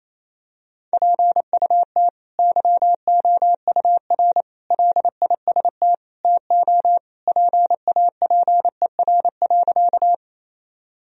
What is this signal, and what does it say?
Morse code, heard as a keyed tone, about 28 wpm, 700 Hz. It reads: PUT YOUR LIST TO PAPER.
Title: Sentences from Top 1000 Words - Review Part 2 - 28wpm